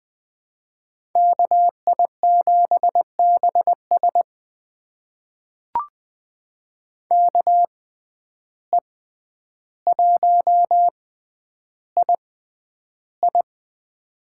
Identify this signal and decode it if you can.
Morse code, heard as a keyed tone, about 21 wpm, spaced as for 10 wpm, 700 Hz. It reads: KI7BS KE1II
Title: USA Call Signs - 5 Characters - 6x Character Spacing - 20wpm